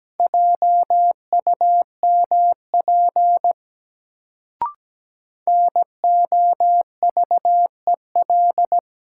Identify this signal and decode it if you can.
Morse code, heard as a keyed tone, about 17 wpm, 700 Hz. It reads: JUMP NOVEL
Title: Sentences from Top 1300 Words - New Word Drill - 17wpm